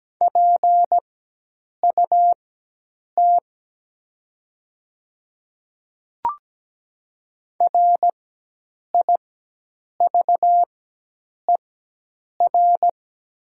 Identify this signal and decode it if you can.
Morse code, heard as a keyed tone, about 17 wpm, spaced as for 8 wpm, 700 Hz. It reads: PUT RIVER
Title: 4x Character Spacing - Top 300 Words - 17wpm